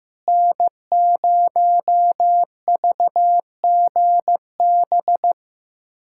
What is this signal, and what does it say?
Morse code, heard as a keyed tone, about 15 wpm, 700 Hz. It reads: N0VGB
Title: USA Callsign DE Callsign 15wpm